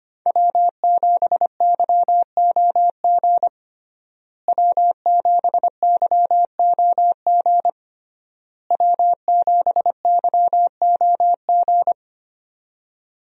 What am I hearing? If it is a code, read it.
Morse code, heard as a keyed tone, about 25 wpm, 700 Hz. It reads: W7YOG W7YOG W7YOG
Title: USA Call Signs - 5 Characters - Repeated 3x - 25wpm